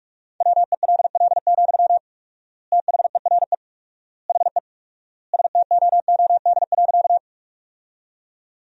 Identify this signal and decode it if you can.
Morse code, heard as a keyed tone, about 46 wpm, 700 Hz. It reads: WELL, THERE HE STOOD.
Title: 4x Word-Spacing - Sentences from Top 500 Words - 45wpm